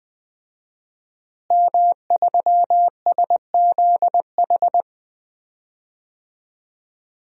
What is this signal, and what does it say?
Morse code, heard as a keyed tone, about 20 wpm, 700 Hz. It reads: M3SZH
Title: DXCC-100 Call Signs - 5 Characters - Repeated 3x - 20wpm